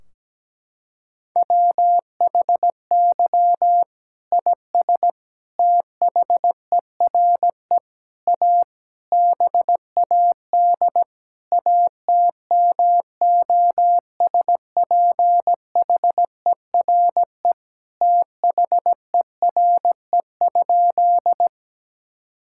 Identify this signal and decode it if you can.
Morse code, heard as a keyed tone, about 17 wpm, 700 Hz. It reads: WHY IS THERE A BAD ATMOSPHERE THERE?
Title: Sentences from Top 1600 Words 17wpm